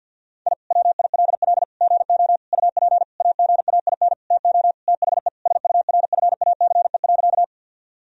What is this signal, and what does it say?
Morse code, heard as a keyed tone, about 50 wpm, 700 Hz. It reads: I WILL GO UP AGAIN TO THE SURFACE.
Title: Sentences from Top 1000 Words - Review Part 3 - 50wpm